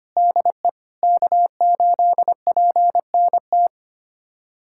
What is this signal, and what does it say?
Morse code, heard as a keyed tone, about 25 wpm, 700 Hz. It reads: DE K8PNT
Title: USA Callsign DE Callsign - 5 Characters - 25wpm